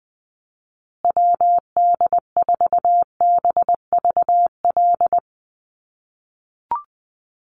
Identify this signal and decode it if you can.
Morse code, heard as a keyed tone, about 20 wpm, 700 Hz. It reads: WD4BVL
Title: USA Call Signs Repeated 2x 20wpm